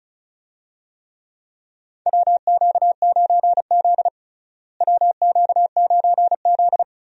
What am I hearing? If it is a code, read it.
Morse code, heard as a keyed tone, about 34 wpm, 700 Hz. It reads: WQ9Z WQ9Z
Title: USA Call Signs - 4 Characters - Repeated 3x - 35wpm